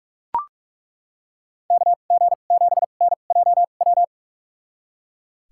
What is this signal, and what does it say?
Morse code, heard as a keyed tone, about 41 wpm, spaced as for 31 wpm, 700 Hz. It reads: KG7NJW
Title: USA Call Signs - 6 Characters - 2x Character Spacing - 45wpm